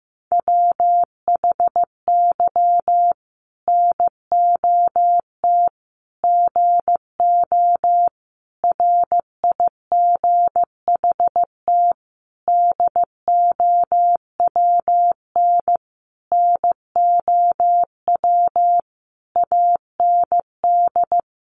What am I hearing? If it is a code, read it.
Morse code, heard as a keyed tone, about 15 wpm, 700 Hz. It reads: WHY NOT GO RIGHT DOWN NOW AND
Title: Sentences from Top 1400 Words 15wpm